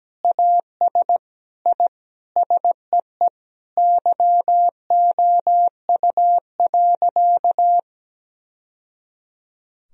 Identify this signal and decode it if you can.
Morse code, heard as a keyed tone, about 17 wpm, 700 Hz. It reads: AS I SEE YOU.
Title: Sentences from Top 1000 Words - Review Part 2 - 17wpm